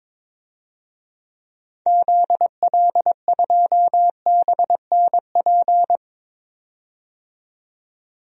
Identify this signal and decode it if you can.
Morse code, heard as a keyed tone, about 22 wpm, 700 Hz. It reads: ZL2BNP
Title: DXCC-100 Call Signs - 6 Characters - 22wpm